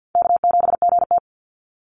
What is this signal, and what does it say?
Morse code, heard as a keyed tone, about 49 wpm, 700 Hz. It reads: K7GT
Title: Easy Call Signs Repeated 2x - 50wpm